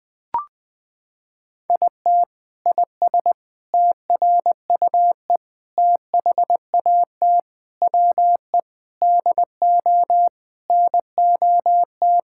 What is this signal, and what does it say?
Morse code, heard as a keyed tone, about 20 wpm, 700 Hz. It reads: IT IS TRUE THAT WE DO NOT
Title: Sentences from Top 1600 Words 20wpm